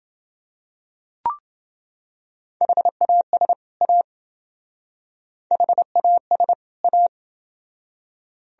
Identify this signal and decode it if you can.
Morse code, heard as a keyed tone, about 28 wpm, 700 Hz. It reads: HAS A HAS A